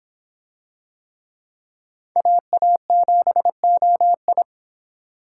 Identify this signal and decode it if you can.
Morse code, heard as a keyed tone, about 25 wpm, 700 Hz. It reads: AA7OI